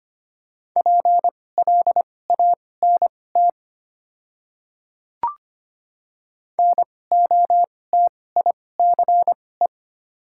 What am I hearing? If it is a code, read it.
Morse code, heard as a keyed tone, about 25 wpm, spaced as for 18 wpm, 700 Hz. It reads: PLANT NOTICE
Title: 2x Character Spacing - Top 500 Words - 25wpm